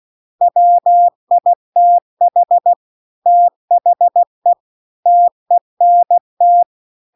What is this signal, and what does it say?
Morse code, heard as a keyed tone, about 16 wpm, 700 Hz. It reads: WITH THE TENT